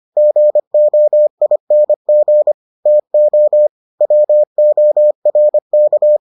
Morse code, heard as a keyed tone, about 25 wpm, 600 Hz. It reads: GOING TO WORK